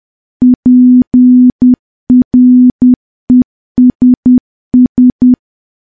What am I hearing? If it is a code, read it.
Morse code, heard as a keyed tone, about 10 wpm, 250 Hz. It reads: PRESS
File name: morse / morse-training-10WPM